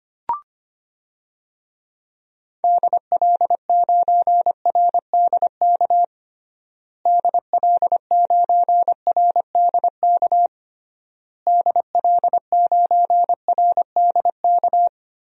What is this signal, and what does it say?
Morse code, heard as a keyed tone, about 25 wpm, 700 Hz. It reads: DL9RDK DL9RDK DL9RDK
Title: DXCC-100 Call Signs - 6 Characters - Repeated 3x - 25wpm